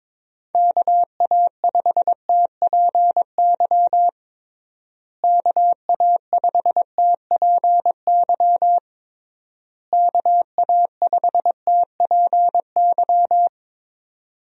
Morse code, heard as a keyed tone, about 22 wpm, 700 Hz. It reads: KA5TPY KA5TPY KA5TPY
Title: USA Call Signs - 6 Characters - Repeated 3x - 22wpm